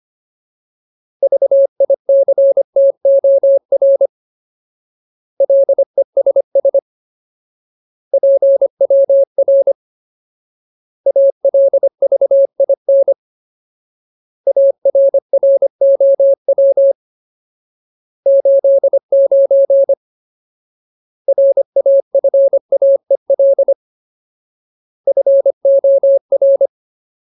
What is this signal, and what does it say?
Morse code, heard as a keyed tone, about 25 wpm, 550 Hz. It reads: VICTOR LESS PWR ALVIN ARROW 89 RAFAEL FOR
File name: Ultimate Word List / 25WPM Ultimate